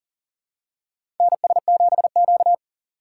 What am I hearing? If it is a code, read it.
Morse code, heard as a keyed tone, about 40 wpm, 700 Hz. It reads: NS7Q
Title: USA Call Signs - 4 Characters - 6x Character Spacing - 40wpm